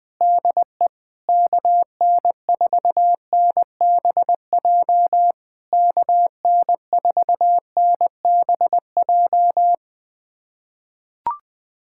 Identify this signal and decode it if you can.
Morse code, heard as a keyed tone, about 20 wpm, 700 Hz. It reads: DE KN4NBJ KN4NBJ
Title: POTA QSO Practice - 20wpm